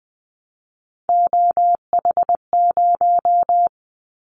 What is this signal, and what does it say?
Morse code, heard as a keyed tone, about 20 wpm, 700 Hz. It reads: OH0
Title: Callsign Prefixes for Top 100 DXCC - 20wpm